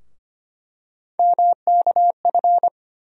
Morse code, heard as a keyed tone, about 25 wpm, 700 Hz. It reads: MKF
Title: USA Call Sign Postfixes - 3 Letters 25wpm